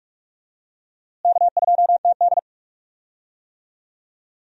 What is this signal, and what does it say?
Morse code, heard as a keyed tone, about 45 wpm, 700 Hz. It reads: K2TD
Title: USA Call Signs - 4 Characters - Repeated 2x - 45wpm